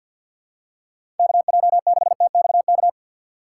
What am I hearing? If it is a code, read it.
Morse code, heard as a keyed tone, about 48 wpm, 700 Hz. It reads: KJ6TXK
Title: USA Call Signs - 6 Characters - 4x Character Spacing - 50wpm